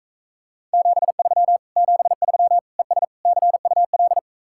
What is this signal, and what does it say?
Morse code, heard as a keyed tone, about 35 wpm, 700 Hz. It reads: 73 73 ES CUL
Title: Speed-Racing Common QSO Phrases Encore - 28wpm